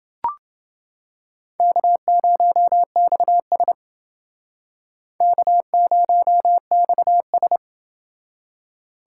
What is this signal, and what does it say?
Morse code, heard as a keyed tone, about 29 wpm, 700 Hz. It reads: K0XS K0XS